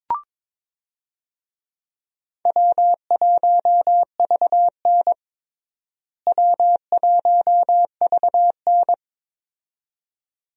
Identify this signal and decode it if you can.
Morse code, heard as a keyed tone, about 21 wpm, 700 Hz. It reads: W1VN W1VN
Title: USA Call Signs - 4 Characters - Repeated 2x - 22wpm